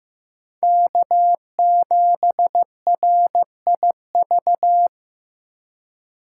Morse code, heard as a keyed tone, about 15 wpm, 700 Hz. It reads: K7RIV
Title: USA Call Signs - 5 Characters - 6x Character Spacing - 15wpm